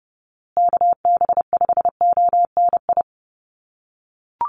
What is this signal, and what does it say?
Morse code, heard as a keyed tone, about 30 wpm, 700 Hz. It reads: KB5ONI
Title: USA Call Signs Repeated 3x 30wpm